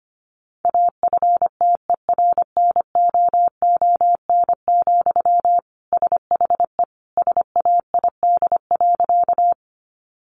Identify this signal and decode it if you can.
Morse code, heard as a keyed tone, about 25 wpm, 700 Hz. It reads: AFTERNOON, SHE SAID.
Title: Sentences from Top 1500 Words 25wpm